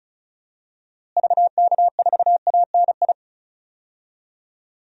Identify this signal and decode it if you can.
Morse code, heard as a keyed tone, about 34 wpm, 700 Hz. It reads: VK4ANI
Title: DXCC-100 Call Signs - 6 Characters - Repeated 2x - 35wpm